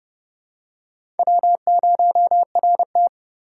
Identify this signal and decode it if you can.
Morse code, heard as a keyed tone, about 30 wpm, 700 Hz. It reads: W0RT
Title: USA Call Signs - 4x Character Spacing - 30wpm